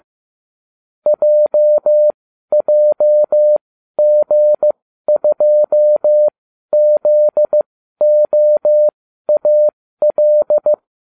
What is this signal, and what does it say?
Morse code, heard as a keyed tone, about 15 wpm, spaced as for 12 wpm, 600 Hz. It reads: JJG2ZOAL